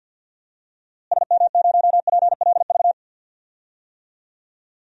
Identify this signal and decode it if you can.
Morse code, heard as a keyed tone, about 51 wpm, 700 Hz. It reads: SM0PLV
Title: DXCC-100 Call Signs - 6 Characters - Repeated 2x - 50wpm